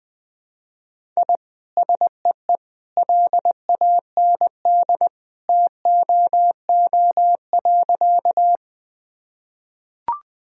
Morse code, heard as a keyed tone, about 20 wpm, 700 Hz. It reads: I SEE LAND TOO.
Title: Sentences from Top 1000 Words - Review Part 1 - 20wpm